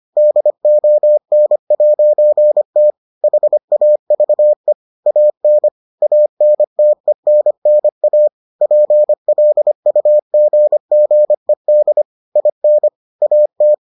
Morse code, heard as a keyed tone, about 25 wpm, 600 Hz. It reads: DON'T HAVE AN ANTENNA PLUGGED IN AT